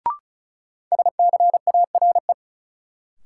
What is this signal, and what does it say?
Morse code, heard as a keyed tone, about 34 wpm, 700 Hz. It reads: SCARE
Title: Sentences from Top 1000 Words - Review Part 5 - New Word Drill - 35wpm